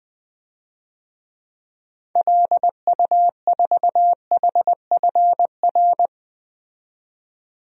Morse code, heard as a keyed tone, about 20 wpm, 700 Hz. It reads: LU4HFR
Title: DXCC-100 Call Signs - 6 Characters - 20wpm